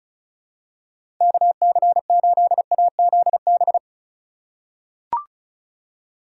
Morse code, heard as a keyed tone, about 34 wpm, 700 Hz. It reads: KC8AZB
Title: USA Call Signs - 6 Characters - 2x Character Spacing - 35wpm